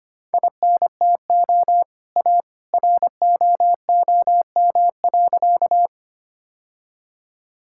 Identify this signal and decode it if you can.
Morse code, heard as a keyed tone, about 25 wpm, 700 Hz. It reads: INTO A ROOM.
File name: practice / 25wpm / Sentences with 2-Word Phrases - 25wpm